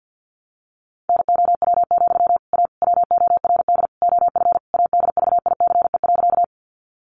Morse code, heard as a keyed tone, about 48 wpm, 700 Hz. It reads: NOW, A WORD OF ADVICE.
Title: Sentences from Top 1100 Words 50wpm